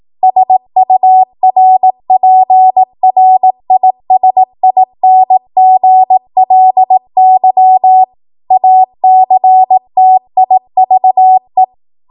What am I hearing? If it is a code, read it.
Morse code, heard as a keyed tone, about 18 wpm, 750 Hz. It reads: SURPRISINGLY ACTIVE
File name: Morse / 18WPM